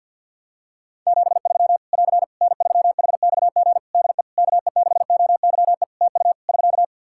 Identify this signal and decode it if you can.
Morse code, heard as a keyed tone, about 51 wpm, 700 Hz. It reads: 73 <AR> N3HXG DE KE6OYE TU <SK>
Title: Practice QSO - Exchange 4 - Ending - 50wpm